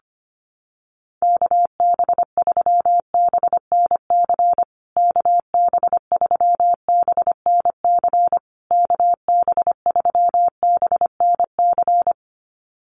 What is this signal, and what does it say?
Morse code, heard as a keyed tone, about 25 wpm, 700 Hz. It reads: KB3BNC KB3BNC KB3BNC
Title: USA Call Signs Repeated 3x 25wpm